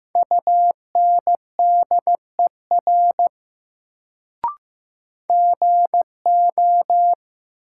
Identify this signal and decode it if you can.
Morse code, heard as a keyed tone, about 15 wpm, 700 Hz. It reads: UNDER GO
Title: Sets of 2 Words - New Word Drill - 15wpm